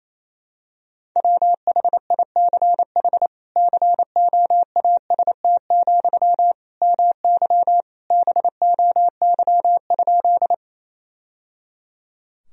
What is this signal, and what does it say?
Morse code, heard as a keyed tone, about 28 wpm, 700 Hz. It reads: WHICH COAST, MY BOY?
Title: Sentences from Top 1000 Words - Review Part 4 - 28wpm